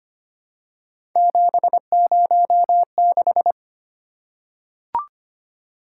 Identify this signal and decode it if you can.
Morse code, heard as a keyed tone, about 25 wpm, 700 Hz. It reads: 706
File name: Three Letter-Number Combinations - 25wpm